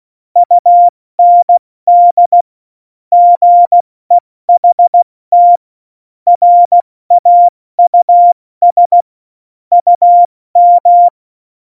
Morse code, heard as a keyed tone, about 16 wpm, spaced as for 14 wpm, 700 Hz. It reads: UND GEHT RAUS UM
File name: Morsekurs / Witze / Witz 29. Januar 2019, 70 F80